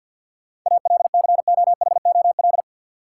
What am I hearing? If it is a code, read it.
Morse code, heard as a keyed tone, about 48 wpm, 700 Hz. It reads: ALCOHOL
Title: Sentences from Top 1400 Words - New Word Drill Alt - 50wpm